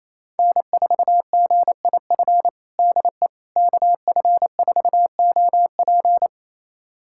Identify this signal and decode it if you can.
Morse code, heard as a keyed tone, about 28 wpm, 700 Hz. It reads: N4GIF DE KF4OP
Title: USA Callsign DE Callsign - 5 Characters - 28wpm